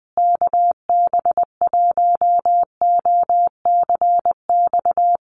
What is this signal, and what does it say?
Morse code, heard as a keyed tone, about 19 wpm, 700 Hz. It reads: KB1OCX
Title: USA Call Signs Repeated 3x 20wpm